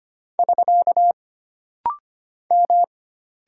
Morse code, heard as a keyed tone, about 25 wpm, 700 Hz. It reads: <SK> M
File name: Single Character (Letter:Number:Punctuation:Prosign) - Rapid-Fire - 25wpm